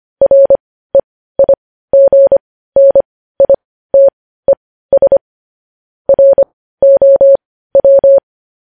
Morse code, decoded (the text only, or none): REIGNITES ROW